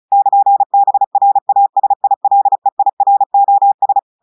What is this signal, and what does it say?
Morse code, heard as a keyed tone, about 36 wpm, 800 Hz. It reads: <KN>BRASILEIROS
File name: audios / portugais / portugais texte en clair 1 vitesse 35 mots minutes